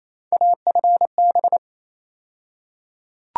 Morse code, heard as a keyed tone, about 28 wpm, 700 Hz. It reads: AFB
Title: USA Call Sign Postfixes - 3 Letters 28wpm